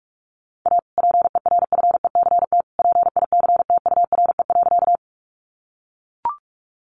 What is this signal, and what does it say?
Morse code, heard as a keyed tone, about 44 wpm, 700 Hz. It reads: A PERFECT PICTURE.
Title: Sentences from Top 1000 Words 45wpm